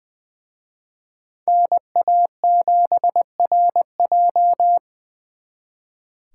Morse code, heard as a keyed tone, about 24 wpm, 700 Hz. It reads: NA7RJ
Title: Speed Racing - USA 2x2 Call Signs 20wpm